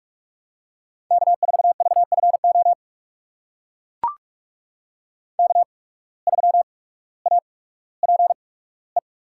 Morse code, heard as a keyed tone, about 42 wpm, spaced as for 19 wpm, 700 Hz. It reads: K4VFO K3APE
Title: USA Call Signs - 5 Characters - 8x Character Spacing - 45wpm